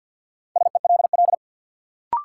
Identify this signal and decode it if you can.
Morse code, heard as a keyed tone, about 53 wpm, 700 Hz. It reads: SELL
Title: Sentences from Top 1000 Words - Review Part 6 - New Word Drill - 50wpm